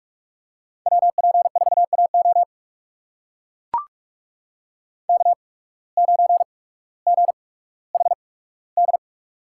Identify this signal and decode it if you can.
Morse code, heard as a keyed tone, about 44 wpm, spaced as for 19 wpm, 690 Hz. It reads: WP4AO K9GHD
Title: USA Call Signs - 8x Character Spacing - 45wpm